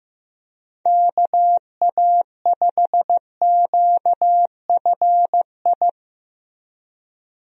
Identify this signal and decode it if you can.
Morse code, heard as a keyed tone, about 15 wpm, 700 Hz. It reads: KA5QFI
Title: USA Call Signs - 6 Characters - Repeated 3x - 15wpm